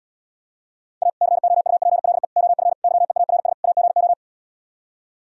Morse code, heard as a keyed tone, about 77 wpm, 700 Hz. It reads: A COUPLE OF YEARS AGO